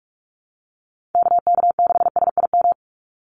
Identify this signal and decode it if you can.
Morse code, heard as a keyed tone, about 44 wpm, 700 Hz. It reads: KK6SIM
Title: USA Call Signs 45wpm